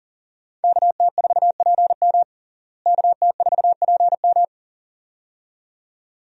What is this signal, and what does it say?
Morse code, heard as a keyed tone, about 37 wpm, 700 Hz. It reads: KT4PM KT4PM
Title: USA Call Signs - 5 Characters - Repeated 2x - 40wpm